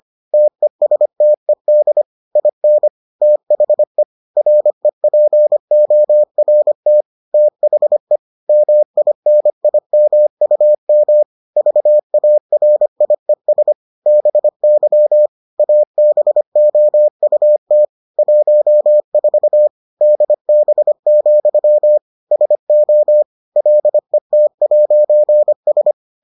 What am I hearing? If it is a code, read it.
Morse code, heard as a keyed tone, about 25 wpm, 600 Hz. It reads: TESTED IN THE REPORT THE MINIMUM VARIES BY ABOUT 14 DB, SO LET'S